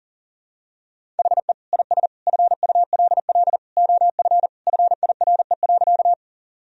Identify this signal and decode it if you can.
Morse code, heard as a keyed tone, about 40 wpm, 700 Hz. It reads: HE IS FULL OF FIRE.
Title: Sentences from Top 1000 Words - Review Part 3 - 40wpm